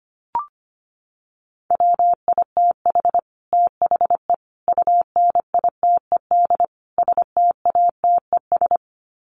Morse code, heard as a keyed tone, about 25 wpm, 700 Hz. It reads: WITH THE UNITED STATES
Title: Sets of 4 Words 25wpm